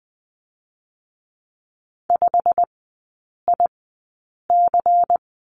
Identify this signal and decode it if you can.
Morse code, heard as a keyed tone, about 20 wpm, spaced as for 8 wpm, 700 Hz. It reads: 5IC
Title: 107 Lesson 18e - Letter m - Three Characters 20wpm